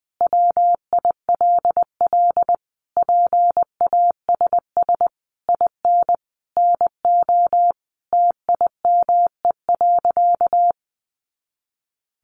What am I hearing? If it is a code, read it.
Morse code, heard as a keyed tone, about 20 wpm, 700 Hz. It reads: WILL PASS IN NO TIME.
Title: Sentences from Top 400 Words 20wpm